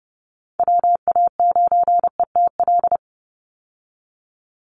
Morse code, heard as a keyed tone, about 30 wpm, 700 Hz. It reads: WA9ETL
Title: USA Call Signs 30wpm